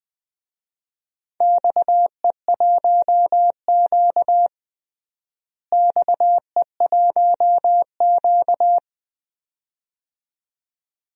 Morse code, heard as a keyed tone, about 21 wpm, 700 Hz. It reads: XE1Q XE1Q
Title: DXCC-100 Call Signs - 4 Characters - Repeated 2x - 20wpm